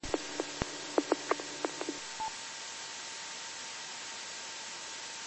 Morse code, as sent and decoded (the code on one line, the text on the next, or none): none